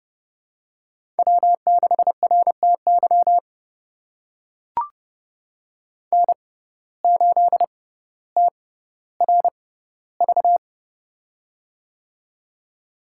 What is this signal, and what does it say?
Morse code, heard as a keyed tone, about 32 wpm, spaced as for 16 wpm, 700 Hz. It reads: W6RTY N8TRV
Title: USA Call Signs - 5 Characters - 6x Character Spacing - 30wpm